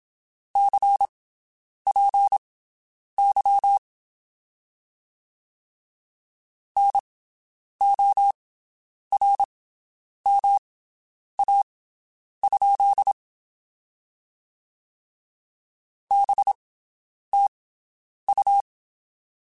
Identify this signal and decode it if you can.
Morse code, heard as a keyed tone, about 27 wpm, spaced as for 9 wpm, 800 Hz. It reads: CPY NORMA? BTU
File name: QSOs Spaced